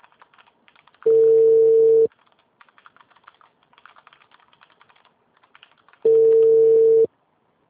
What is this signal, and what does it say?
Ringback tone (China, 450 Hz).
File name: out- -2099- - - .5965